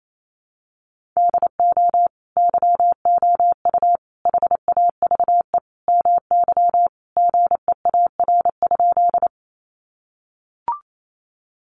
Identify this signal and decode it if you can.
Morse code, heard as a keyed tone, about 28 wpm, 700 Hz. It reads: DO YOU HAVE MY GEAR?
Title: Sentences from Top 1400 Words 28wpm